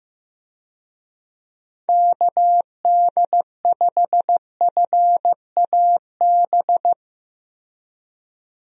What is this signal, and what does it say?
Morse code, heard as a keyed tone, about 15 wpm, 700 Hz. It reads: KD5FAB
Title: USA Call Signs - 6 Characters - Repeated 2x - 15wpm